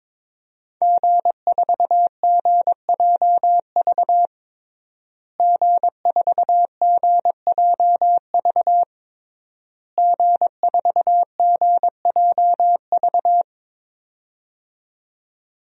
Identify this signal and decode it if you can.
Morse code, heard as a keyed tone, about 22 wpm, 700 Hz. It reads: G4GJV G4GJV G4GJV
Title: DXCC-100 Call Signs - 5 Characters - Repeated 3x - 22wpm